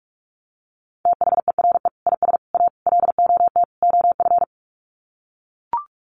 Morse code, heard as a keyed tone, about 44 wpm, 700 Hz. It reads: THERE IS A LOT OF
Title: Sets of 5 Words 45wpm